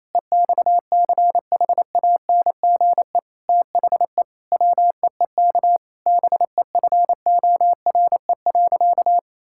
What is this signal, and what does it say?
Morse code, heard as a keyed tone, about 28 wpm, 700 Hz. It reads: EXCHANGE THE WEEK BEFORE.
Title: Sentences from Top 1000 Words - Review Part 6 - 28wpm